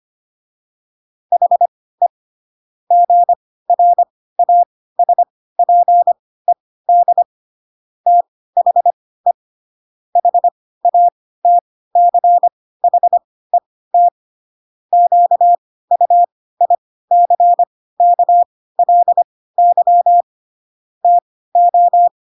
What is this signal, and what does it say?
Morse code, heard as a keyed tone, about 25 wpm, spaced as for 16 wpm, 700 Hz. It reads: HE GRASPED THE HATCHET QUICKLY TO
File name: Pinocchio / Pinocchio ch1 10 25 ik1hge